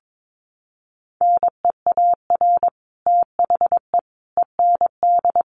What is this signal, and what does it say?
Morse code, heard as a keyed tone, about 22 wpm, 700 Hz. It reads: NEAR THE END
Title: Sets of 5 Words Encore 22wpm